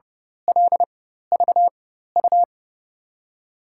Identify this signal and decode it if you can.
Morse code, heard as a keyed tone, about 30 wpm, spaced as for 14 wpm, 700 Hz. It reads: LVU
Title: USA Call Signs - 6 Characters - 4x Character Spacing - 30wpm